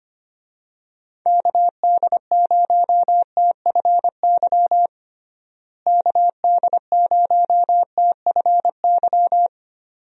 Morse code, heard as a keyed tone, about 25 wpm, 700 Hz. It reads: KD0TFY KD0TFY
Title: USA Call Signs - 6 Characters - Repeated 2x - 25wpm